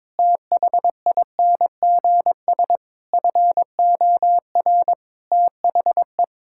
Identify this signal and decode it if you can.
Morse code, heard as a keyed tone, about 22 wpm, 700 Hz. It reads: THINGS FOR THE
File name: Difficult Sentences with Hints - 22wpm